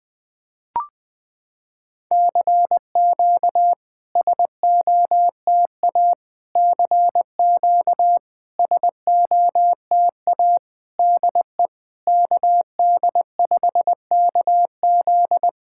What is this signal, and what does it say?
Morse code, heard as a keyed tone, about 20 wpm, 700 Hz. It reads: CQ SOTA CQ SOTA DE KD5KZ
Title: SOTA QSO Practice 20wpm